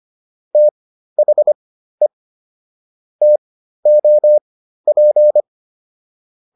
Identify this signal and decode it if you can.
Morse code, heard as a keyed tone, about 25 wpm, spaced as for 13 wpm, 600 Hz. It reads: THE TOP